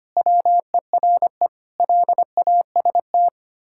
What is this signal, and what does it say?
Morse code, heard as a keyed tone, about 25 wpm, 700 Hz. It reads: WERE LAST
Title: Pangram Challenge v2 - 25wpm